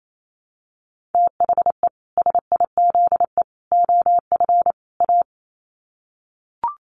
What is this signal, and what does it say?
Morse code, heard as a keyed tone, about 28 wpm, 700 Hz. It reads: THE SIZE OF A
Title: Sets of 4 Words 28wpm